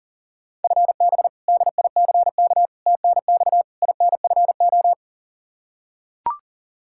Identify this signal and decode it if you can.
Morse code, heard as a keyed tone, about 40 wpm, 700 Hz. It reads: FB DICK TNX INFO
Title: Practice QSO - Exchange 3 - More Info - 40wpm